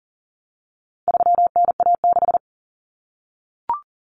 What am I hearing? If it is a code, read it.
Morse code, heard as a keyed tone, about 40 wpm, 700 Hz. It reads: 3NA6
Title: Common Cut Numbers - 4-Digit 40wpm